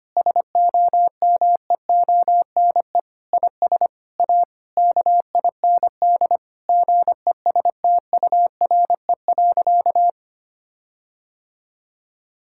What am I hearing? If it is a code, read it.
Morse code, heard as a keyed tone, about 25 wpm, 700 Hz. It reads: SOMEONE IS A KIND GESTURE.
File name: practice / 25wpm / Difficult Sentences with Hints - 25wpm